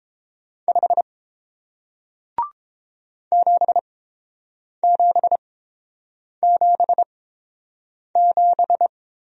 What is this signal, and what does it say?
Morse code, heard as a keyed tone, about 26 wpm, spaced as for 7 wpm, 700 Hz. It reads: 5 7777
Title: Speed-Racing Single Letter-Number 22wpm